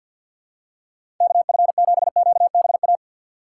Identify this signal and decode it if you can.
Morse code, heard as a keyed tone, about 53 wpm, 700 Hz. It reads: KF7QBA
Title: USA Call Signs - 6 Characters - 4x Character Spacing - 50wpm